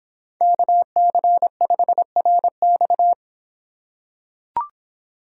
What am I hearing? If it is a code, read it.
Morse code, heard as a keyed tone, about 26 wpm, 700 Hz. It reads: KC5RX